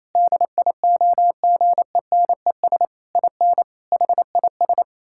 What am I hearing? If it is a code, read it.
Morse code, heard as a keyed tone, about 27 wpm, 700 Hz. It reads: DIOGENES IN HIS
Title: Pangram Challenge v2 - 28wpm